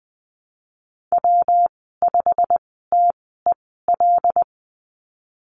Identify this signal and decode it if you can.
Morse code, heard as a keyed tone, about 20 wpm, spaced as for 15 wpm, 700 Hz. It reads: W5TEL
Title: 167 Lesson 27g - Number 7 - Call Signs 20wpm